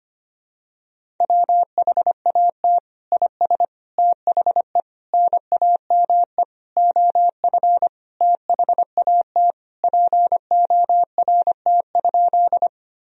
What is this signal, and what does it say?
Morse code, heard as a keyed tone, about 25 wpm, 700 Hz. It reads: WHAT IS THE NAME OF THAT PORT?
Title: Sentences from Top 1000 Words - Review Part 1 - 25wpm